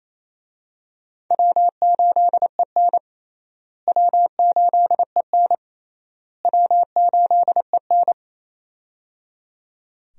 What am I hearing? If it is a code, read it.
Morse code, heard as a keyed tone, about 28 wpm, 700 Hz. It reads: W8EN W8EN W8EN
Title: USA Call Signs - 4 Characters - Repeated 3x - 28wpm